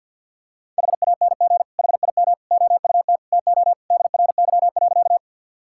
Morse code, heard as a keyed tone, about 50 wpm, 700 Hz. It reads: HANG HIM OUT TO DRY.
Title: Sentences from Top 1600 Words 50wpm